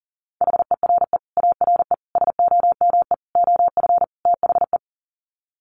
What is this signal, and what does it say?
Morse code, heard as a keyed tone, about 40 wpm, 700 Hz. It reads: HERE ARE SOME OF THE
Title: Sets of 5 Words Encore 40wpm